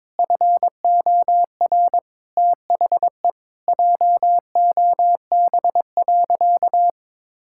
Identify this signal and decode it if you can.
Morse code, heard as a keyed tone, about 21 wpm, 700 Hz. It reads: FOR THE JOB.
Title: Sentences from Top 1600 Words 22wpm